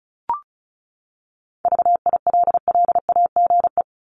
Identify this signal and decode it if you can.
Morse code, heard as a keyed tone, about 34 wpm, 700 Hz. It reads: VILLAGE
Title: Top 1751-2000 Words 35wpm